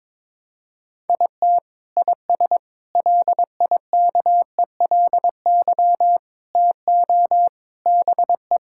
Morse code, heard as a keyed tone, about 22 wpm, 700 Hz. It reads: IT IS LIKELY TO BE
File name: practice / 22wpm / Sentences with 3-Word Phrases - 22wpm